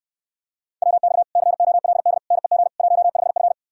Morse code, heard as a keyed tone, about 69 wpm, 700 Hz. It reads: WX COLD NEG 15F